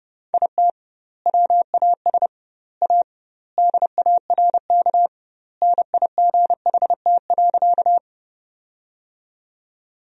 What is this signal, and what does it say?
Morse code, heard as a keyed tone, about 30 wpm, 700 Hz. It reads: IT WAS A DARK NIGHT.
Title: 2x Word-Spacing - Sentences from Top 500 Words - 30wpm